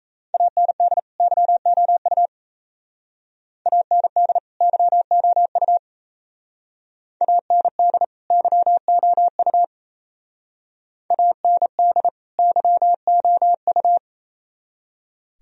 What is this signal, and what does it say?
Morse code, heard as a keyed tone, about 35 wpm, 700 Hz. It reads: AND YOU AND YOU AND YOU AND YOU
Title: Speed-Racing - Sets of 2 Words 28wpm